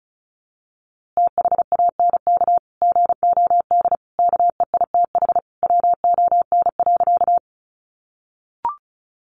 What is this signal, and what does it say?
Morse code, heard as a keyed tone, about 34 wpm, 700 Hz. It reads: THANK GOD KEITH WON.
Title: A Sound Of Thunder 35wpm